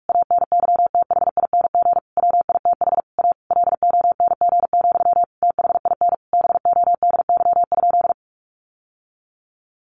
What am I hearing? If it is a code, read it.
Morse code, heard as a keyed tone, about 44 wpm, 700 Hz. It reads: ANYTHING WITH A LONG, THIN BODY?
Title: Sentences from Top 1200 Words 45wpm